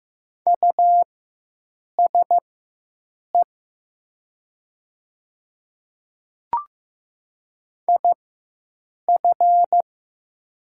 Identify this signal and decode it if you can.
Morse code, heard as a keyed tone, about 15 wpm, spaced as for 7 wpm, 700 Hz. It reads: USE IF
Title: 4x Character Spacing - Top 100 Words - 15wpm